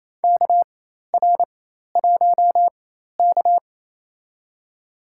Morse code, heard as a keyed tone, about 28 wpm, spaced as for 13 wpm, 700 Hz. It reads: KR1K
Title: USA Call Signs - 4 Characters - 4x Character Spacing - 28wpm